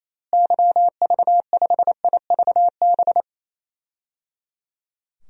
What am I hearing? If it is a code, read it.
Morse code, heard as a keyed tone, about 28 wpm, 700 Hz. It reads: YV5IVB
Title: DXCC-100 Call Signs - 6 Characters - Repeated 3x - 28wpm